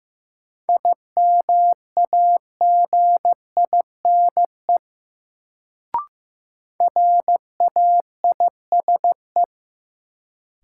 Morse code, heard as a keyed tone, about 15 wpm, 700 Hz. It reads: IMAGINE RAISE
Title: Sentences from Top 1400 Words - New Word Drill - 15wpm